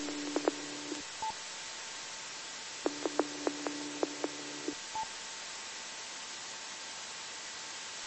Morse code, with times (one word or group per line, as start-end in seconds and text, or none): none